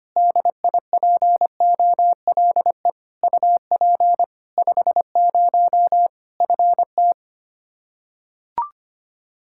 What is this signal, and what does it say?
Morse code, heard as a keyed tone, about 25 wpm, 700 Hz. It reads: DIPOLE UP 50 FT
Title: Practice QSO - Exchange 2 - Info - 25wpm